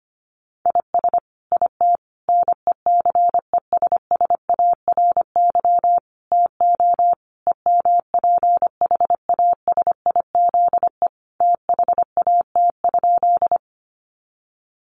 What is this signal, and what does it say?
Morse code, heard as a keyed tone, about 25 wpm, 700 Hz. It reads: IS IT NECESSARY TO EMPHASIZE THAT?
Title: Sentences from Top 1500 Words 25wpm